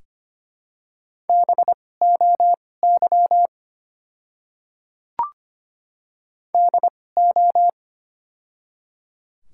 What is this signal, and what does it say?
Morse code, heard as a keyed tone, about 25 wpm, spaced as for 18 wpm, 700 Hz. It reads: BOY DO
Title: 2x Character Spacing - Top 200 Words - 25wpm